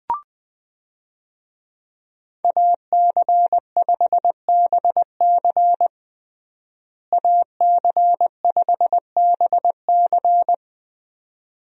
Morse code, heard as a keyed tone, about 20 wpm, 700 Hz. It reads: AC5BC AC5BC
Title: USA Call Signs - 5 Characters - Repeated 2x - 20wpm